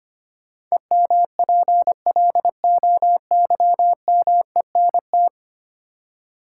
Morse code, heard as a keyed tone, about 25 wpm, 700 Hz. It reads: EMPLOYMENT
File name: Ten Letter Words - 25wpm